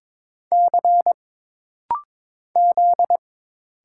Morse code, heard as a keyed tone, about 22 wpm, 700 Hz. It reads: C Z
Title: Single Letters - Rapid-Fire 22wpm